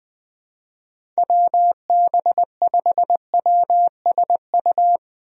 Morse code, heard as a keyed tone, about 21 wpm, 700 Hz. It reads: WB5WSU
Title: USA Call Signs - 6 Characters - Repeated 2x - 20wpm